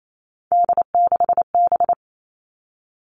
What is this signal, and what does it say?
Morse code, heard as a keyed tone, about 28 wpm, 700 Hz. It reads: D6B
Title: Three Challenging Letter-Number Combinations 28wpm